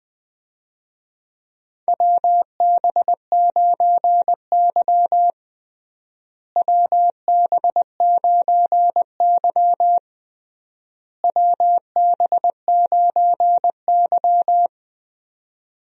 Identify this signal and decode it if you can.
Morse code, heard as a keyed tone, about 20 wpm, 700 Hz. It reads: WB9Y WB9Y WB9Y
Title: USA Call Signs - 4 Characters - Repeated 3x - 20wpm